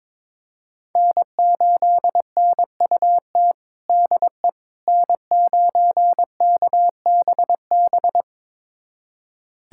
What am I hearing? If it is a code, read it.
Morse code, heard as a keyed tone, about 21 wpm, 700 Hz. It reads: N8NUT DE N9KBB
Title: USA Callsign DE Callsign - 5 Characters - 22wpm